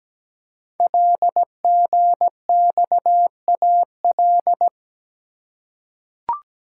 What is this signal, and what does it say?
Morse code, heard as a keyed tone, about 17 wpm, 700 Hz. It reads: LGXAL
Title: Code Groups - 5 Letters - 17wpm